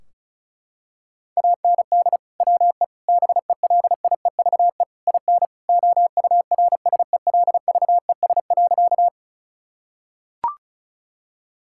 Morse code, heard as a keyed tone, about 34 wpm, 700 Hz. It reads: AND WE BELIEVE IN OURSELVES.
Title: Sentences from Top 1600 Words 35wpm